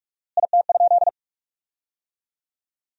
Morse code, heard as a keyed tone, about 47 wpm, 700 Hz. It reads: IT?
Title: 4x Word-Spacing - Sentences from Top 100 Words - 45wpm